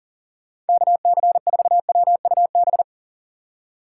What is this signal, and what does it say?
Morse code, heard as a keyed tone, about 51 wpm, 700 Hz. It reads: KC4WUB